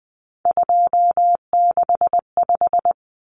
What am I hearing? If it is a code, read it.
Morse code, heard as a keyed tone, about 20 wpm, 700 Hz. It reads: 265
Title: Numbers - 3-Digit 20wpm